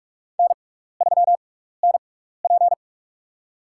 Morse code, heard as a keyed tone, about 42 wpm, spaced as for 15 wpm, 700 Hz. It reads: N3NP
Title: USA Call Signs - 4 Characters - 6x Character Spacing - 45wpm